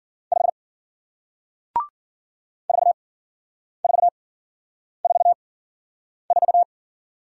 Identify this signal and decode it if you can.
Morse code, heard as a keyed tone, about 50 wpm, spaced as for 9 wpm, 700 Hz. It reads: 5 4444